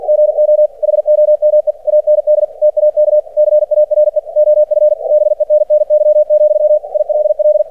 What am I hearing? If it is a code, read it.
Morse code, heard as a keyed tone, about 49 wpm, 600 Hz. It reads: GO HOME AND TAK CARE OF BENNY, ILL
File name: archived / CRSnet